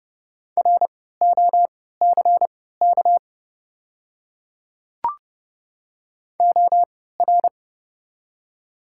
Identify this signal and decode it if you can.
Morse code, heard as a keyed tone, about 29 wpm, spaced as for 17 wpm, 700 Hz. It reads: ROCK OR
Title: 3x Character Spacing - Top 500 Words - 30wpm